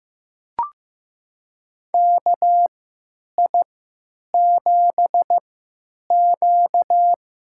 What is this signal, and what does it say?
Morse code, heard as a keyed tone, about 15 wpm, spaced as for 9 wpm, 700 Hz. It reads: KI7Q
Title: USA Call Signs - 3x Character Spacing - 15wpm